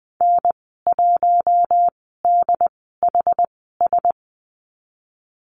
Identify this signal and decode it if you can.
Morse code, heard as a keyed tone, about 20 wpm, spaced as for 15 wpm, 700 Hz. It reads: N1DHS
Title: 088 Lesson 15g - Number 5 - Call Signs 20wpm